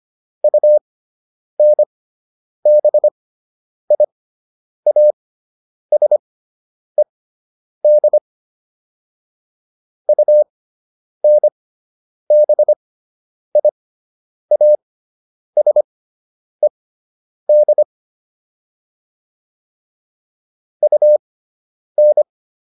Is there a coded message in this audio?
Yes: Morse code from 0.4 to 22.2 s, heard as a keyed tone, about 25 wpm, spaced as for 9 wpm, 600 Hz. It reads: UNBIASED UNBIASED UN